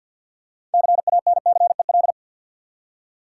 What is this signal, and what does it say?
Morse code, heard as a keyed tone, about 49 wpm, 700 Hz. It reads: CANCEL
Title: Six Letter Words - 50wpm